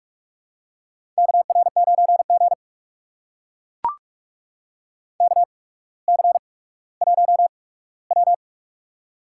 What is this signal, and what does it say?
Morse code, heard as a keyed tone, about 44 wpm, spaced as for 19 wpm, 700 Hz. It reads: KR9G KC1W
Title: USA Call Signs - 4 Characters - 8x Character Spacing - 45wpm